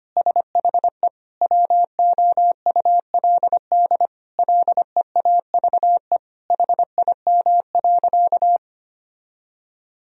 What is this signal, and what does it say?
Morse code, heard as a keyed tone, about 25 wpm, 700 Hz. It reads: SHE WOULD LEAVE HIM.
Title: Sentences from Top 1000 Words - Review Part 4 - 25wpm